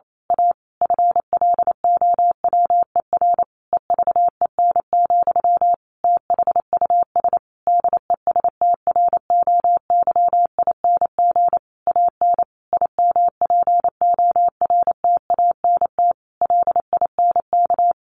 Morse code, heard as a keyed tone, about 28 wpm, 700 Hz. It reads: A FLOWER EVEN, THUS DESTROYING AN IMPORTANT LINK
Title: A Sound Of Thunder 28wpm